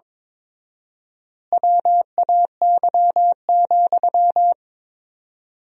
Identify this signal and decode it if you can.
Morse code, heard as a keyed tone, about 22 wpm, 700 Hz. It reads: WAY,
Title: 4x Word-Spacing - Sentences from Top 400 Words - 22wpm